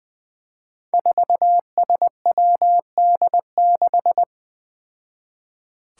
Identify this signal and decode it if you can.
Morse code, heard as a keyed tone, about 20 wpm, 700 Hz. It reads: 4SWD6
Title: Five Challenging Letter-Number Combinations 20wpm